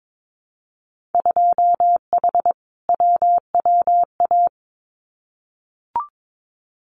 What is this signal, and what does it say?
Morse code, heard as a keyed tone, about 22 wpm, 700 Hz. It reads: 2H WWA
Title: Winter Field Day Exchanges 22wpm